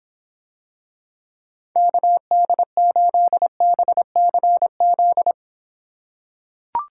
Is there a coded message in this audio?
Yes: Morse code from 1.8 to 5.3 s, heard as a keyed tone, about 26 wpm, 700 Hz. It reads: KD8BCZ